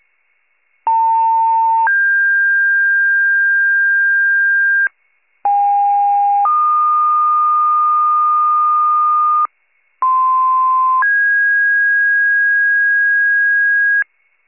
Two-tone page, heard as a one-second tone, then a three-second tone, about 900 then 1600 Hz, about 810 then 1220 Hz, about 1040 then 1650 Hz, with noise. It is heard from 0.9 to 14.0 s.